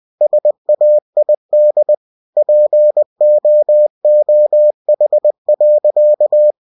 Morse code, heard as a keyed tone, about 20 wpm, 600 Hz. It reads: SAID POOH.
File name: cw-books / winnie-the-pooh / winnie-20wpm-page-0013